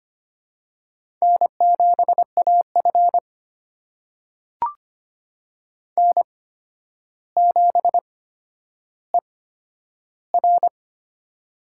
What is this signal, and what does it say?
Morse code, heard as a keyed tone, about 25 wpm, spaced as for 11 wpm, 700 Hz. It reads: N7AF N7ER
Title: USA Call Signs - 4 Characters - 8x Character Spacing - 25wpm